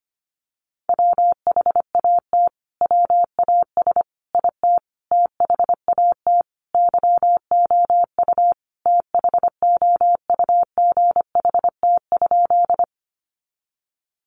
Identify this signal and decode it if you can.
Morse code, heard as a keyed tone, about 25 wpm, 700 Hz. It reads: WHAT WAS IT THAT YOU THOUGHT?
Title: Sentences from Top 300 Words 25wpm